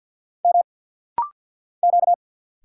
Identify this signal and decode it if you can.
Morse code, heard as a keyed tone, about 49 wpm, 700 Hz. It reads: M Q
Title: Single Letter-Number - Rapid-Fire 50wpm